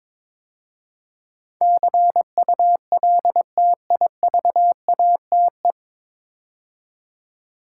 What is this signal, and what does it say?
Morse code, heard as a keyed tone, about 22 wpm, 700 Hz. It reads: CULTIVATE
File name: 9 Letter Words - First 3 Letters Given - 22wpm